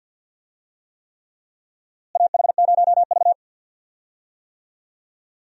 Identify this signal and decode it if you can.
Morse code, heard as a keyed tone, about 47 wpm, 700 Hz. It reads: AH0V
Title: USA Call Signs - 4 Characters - Repeated 3x - 50wpm